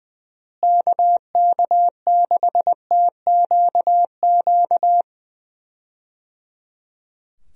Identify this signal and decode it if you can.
Morse code, heard as a keyed tone, about 20 wpm, 700 Hz. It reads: KK6TQQ
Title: USA Call Signs - 6 Characters - Repeated 3x - 20wpm